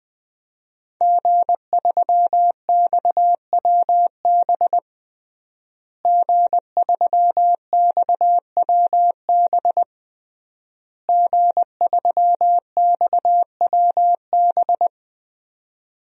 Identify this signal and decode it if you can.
Morse code, heard as a keyed tone, about 20 wpm, 700 Hz. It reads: G3XWB G3XWB G3XWB
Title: DXCC-100 Call Signs - 5 Characters - Repeated 3x - 20wpm